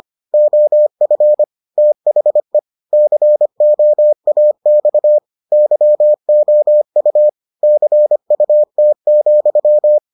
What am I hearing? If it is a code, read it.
Morse code, heard as a keyed tone, about 25 wpm, 600 Hz. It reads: OF THE COAX YOU CUT,